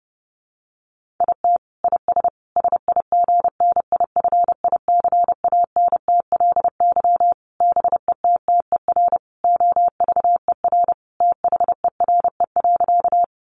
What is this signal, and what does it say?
Morse code, heard as a keyed tone, about 30 wpm, 700 Hz. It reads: IT IS SIGNIFICANTLY BETTER OVER THERE.
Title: Sentences from Top 1500 Words 30wpm